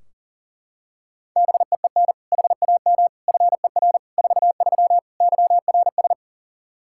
Morse code, heard as a keyed tone, about 40 wpm, 700 Hz. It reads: BEEN HAM FER 43 YRS
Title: Practice QSO - Exchange 3 - More Info - 40wpm